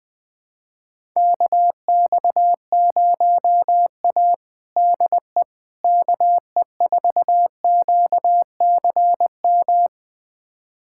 Morse code, heard as a keyed tone, about 20 wpm, 700 Hz. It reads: KX0A DE KE4QCM
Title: USA Callsign DE Callsign 20wpm